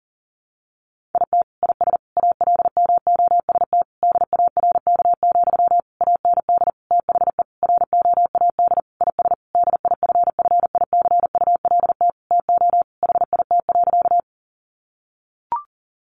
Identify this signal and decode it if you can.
Morse code, heard as a keyed tone, about 40 wpm, 700 Hz. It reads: IT IS ALMOST DARK, AND THE ROAD IS DIFFICULT TO HIT.
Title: Sentences from Top 1200 Words 40wpm